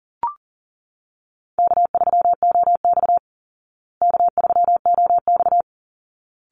Morse code, heard as a keyed tone, about 40 wpm, 700 Hz. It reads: K3OX K3OX
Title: Easy Call Signs Repeated 2x - 40wpm